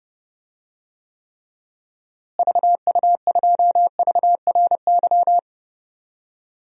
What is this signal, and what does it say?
Morse code, heard as a keyed tone, about 30 wpm, 700 Hz. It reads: VU2VRY
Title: DXCC-100 Call Signs - 6 Characters - Repeated 2x - 30wpm